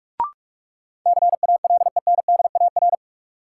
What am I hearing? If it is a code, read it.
Morse code, heard as a keyed tone, about 46 wpm, 700 Hz. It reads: CALENDAR
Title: Sentences from Top 1600 Words - New Word Drill - 45wpm